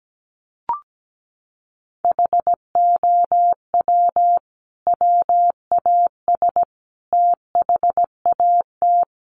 Morse code, heard as a keyed tone, about 17 wpm, 700 Hz. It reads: HOW WAS THAT
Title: Sentences from Top 500 Words 17wpm